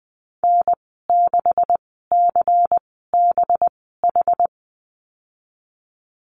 Morse code, heard as a keyed tone, about 20 wpm, spaced as for 14 wpm, 700 Hz. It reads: N6CBH
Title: 238 Lesson 38g - Letter q - Call Signs 20wpm